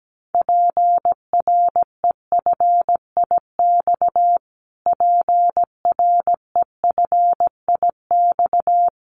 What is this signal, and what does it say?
Morse code, heard as a keyed tone, about 17 wpm, 700 Hz. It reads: PREFIX PREFIX
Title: Callsign Prefixes for Top 25 DXCC - Repeated 3x - 17wpm